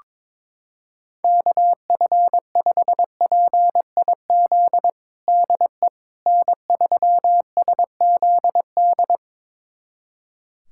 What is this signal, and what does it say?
Morse code, heard as a keyed tone, about 22 wpm, 700 Hz. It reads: KF5PIZ DE N3SZD
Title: USA Callsign DE Callsign 22wpm